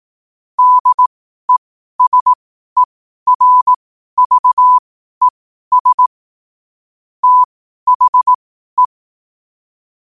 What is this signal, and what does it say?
Morse code, heard as a keyed tone, about 18 wpm, spaced as for 12 wpm, 1000 Hz. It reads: DESERVES THE
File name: qotdfiles / 12wpm